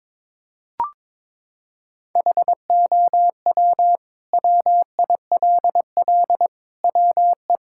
Morse code, heard as a keyed tone, about 22 wpm, 700 Hz. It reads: HOW WILL WE
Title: Sentences from Top 1000 Words - Review Part 6 - 22wpm